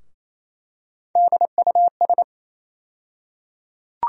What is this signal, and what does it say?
Morse code, heard as a keyed tone, about 28 wpm, 700 Hz. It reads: DUS
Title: USA Call Sign Postfixes - 3 Letters 28wpm